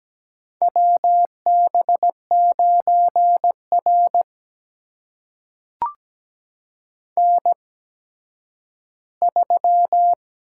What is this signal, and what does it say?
Morse code, heard as a keyed tone, about 17 wpm, spaced as for 10 wpm, 700 Hz. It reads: WB9R N3
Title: USA Call Signs - 8x Character Spacing - 17wpm